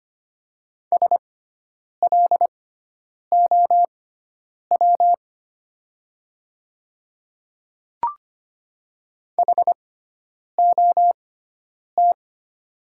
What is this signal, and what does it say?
Morse code, heard as a keyed tone, about 25 wpm, spaced as for 9 wpm, 700 Hz. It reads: SLOW HOT
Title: 6x Character Spacing - Top 500 Words - 25wpm